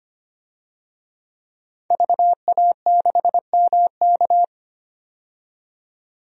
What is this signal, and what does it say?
Morse code, heard as a keyed tone, about 25 wpm, 700 Hz. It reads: VA6MK
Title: DXCC-100 Call Signs - 5 Characters - 25wpm